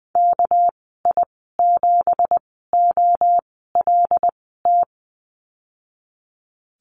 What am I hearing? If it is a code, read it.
Morse code, heard as a keyed tone, about 19 wpm, spaced as for 14 wpm, 700 Hz. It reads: KI7OLT
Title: 207 Lesson 33g - Letter j - Call Signs 20wpm